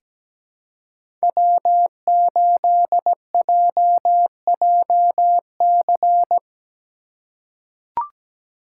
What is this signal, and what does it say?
Morse code, heard as a keyed tone, about 17 wpm, 700 Hz. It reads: W8JJC
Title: USA Call Signs - 5 Characters - 3x Character Spacing - 17wpm